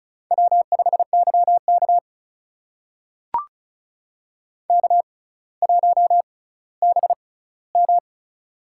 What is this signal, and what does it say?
Morse code, heard as a keyed tone, about 36 wpm, spaced as for 18 wpm, 700 Hz. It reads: W5YK K1BM
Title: USA Call Signs - 4 Characters - 6x Character Spacing - 35wpm